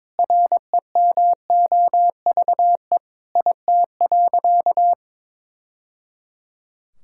Morse code, heard as a keyed tone, about 22 wpm, 700 Hz. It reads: REMOVE IT.
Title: Sentences from Top 1000 Words - Review Part 5 - 22wpm